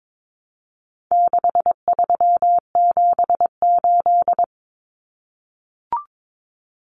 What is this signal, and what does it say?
Morse code, heard as a keyed tone, about 22 wpm, 700 Hz. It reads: 6378
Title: Numbers - 4-Digit 22wpm